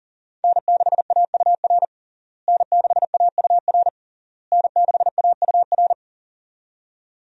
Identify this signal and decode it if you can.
Morse code, heard as a keyed tone, about 40 wpm, 700 Hz. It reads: N6AUR N6AUR N6AUR
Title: USA Call Signs - 5 Characters - Repeated 3x - 40wpm